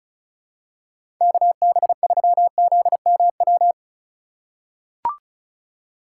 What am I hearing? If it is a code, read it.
Morse code, heard as a keyed tone, about 36 wpm, 700 Hz. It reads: KB3ZMW
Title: USA Call Signs - 2x Character Spacing - 35wpm